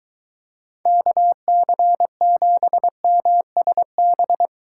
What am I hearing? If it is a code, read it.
Morse code, heard as a keyed tone, about 23 wpm, 700 Hz. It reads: KC7MSB